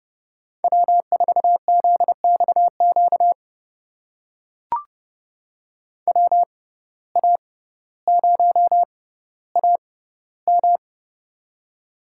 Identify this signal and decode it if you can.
Morse code, heard as a keyed tone, about 30 wpm, spaced as for 15 wpm, 690 Hz. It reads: W4ZXQ WA0AM
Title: USA Call Signs - 5 Characters - 6x Character Spacing - 30wpm